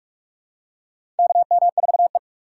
Morse code, heard as a keyed tone, about 42 wpm, 700 Hz. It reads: KM4E
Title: USA Call Signs - 4 Characters - 4x Character Spacing - 45wpm